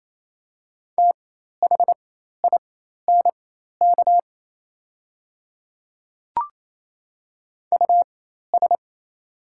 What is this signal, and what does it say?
Morse code, heard as a keyed tone, about 28 wpm, spaced as for 13 wpm, 700 Hz. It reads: THINK US
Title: 4x Character Spacing - Top 200 Words - 28wpm